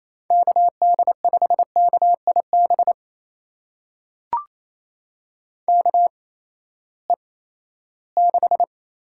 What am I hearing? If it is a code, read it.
Morse code, heard as a keyed tone, about 27 wpm, spaced as for 16 wpm, 700 Hz. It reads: KD5KIB KE6
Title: USA Call Signs - 6 Characters - 8x Character Spacing - 28wpm